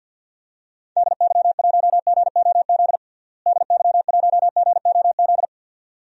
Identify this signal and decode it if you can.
Morse code, heard as a keyed tone, about 53 wpm, 700 Hz. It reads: DY1GOZ DY1GOZ
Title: DXCC-100 Call Signs - 6 Characters - Repeated 2x - 50wpm